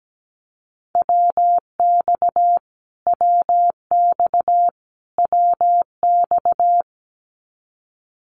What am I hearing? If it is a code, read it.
Morse code, heard as a keyed tone, about 17 wpm, 700 Hz. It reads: WX WX WX
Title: Abbreviations Repeated 3x - 17wpm